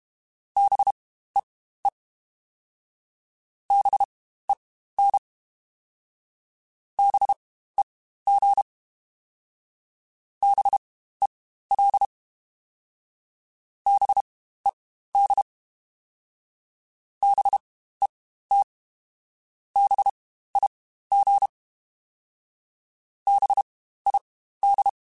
Morse code, heard as a keyed tone, about 32 wpm, spaced as for 15 wpm, 800 Hz. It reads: BEE BEN BEG BEL BED BET BIG BID
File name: Spaced Alphas / Spaced Alphas mp3s / Bs sp @30